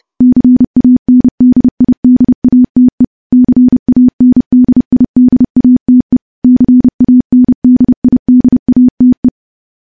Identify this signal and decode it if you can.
Morse code, heard as a keyed tone, about 28 wpm, 250 Hz. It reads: CANDIDATE CANDIDATE CANDIDATE
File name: morse-training-30WPM